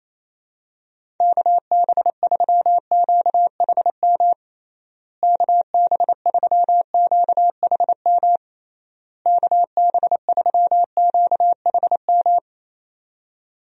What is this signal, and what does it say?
Morse code, heard as a keyed tone, about 27 wpm, 700 Hz. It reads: KB3QHM KB3QHM KB3QHM
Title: USA Call Signs - 6 Characters - Repeated 3x - 28wpm